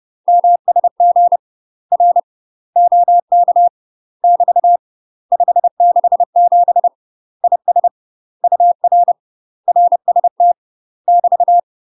Morse code, heard as a keyed tone, about 30 wpm, 700 Hz. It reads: MSG R OK <BT> 567 IS UR RST <BT>
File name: UK MASTER QSOs TRAINING / 30WPM / UK MASTER QSOs 0035